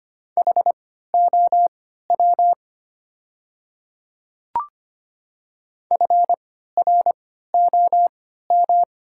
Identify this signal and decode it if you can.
Morse code, heard as a keyed tone, about 25 wpm, spaced as for 14 wpm, 690 Hz. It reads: HOW FROM
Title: 3x Character Spacing - Top 100 Words - 25wpm